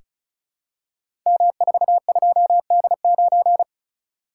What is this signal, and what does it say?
Morse code, heard as a keyed tone, about 36 wpm, 690 Hz. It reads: M42D9
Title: Code Groups - 5 Letter:Numbers - 35wpm